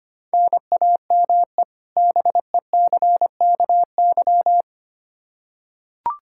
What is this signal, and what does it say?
Morse code, heard as a keyed tone, about 25 wpm, 700 Hz. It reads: NAME BECKY
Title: Practice QSO - Exchange 1 - Beginning - 25wpm